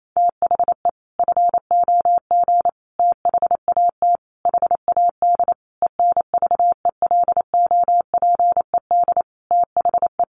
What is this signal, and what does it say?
Morse code, heard as a keyed tone, about 28 wpm, 700 Hz. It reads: THE FOG THAT HAD ENVELOPED THE
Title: A Sound Of Thunder 28wpm